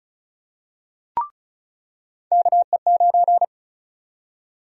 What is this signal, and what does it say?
Morse code, heard as a keyed tone, about 34 wpm, 700 Hz. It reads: KE9
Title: USA Call Sign Prefixes - 35wpm